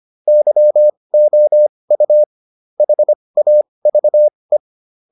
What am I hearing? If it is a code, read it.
Morse code, heard as a keyed tone, about 25 wpm, spaced as for 20 wpm, 600 Hz. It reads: YOU HAVE